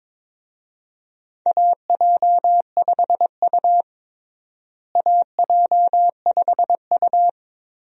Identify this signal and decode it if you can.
Morse code, heard as a keyed tone, about 22 wpm, 700 Hz. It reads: AJ5U AJ5U
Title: USA Call Signs - 4 Characters - Repeated 3x - 22wpm